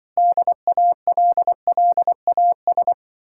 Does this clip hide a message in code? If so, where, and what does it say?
Yes: Morse code from 0.2 to 2.9 s, heard as a keyed tone, about 25 wpm, 700 Hz. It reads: DALLAS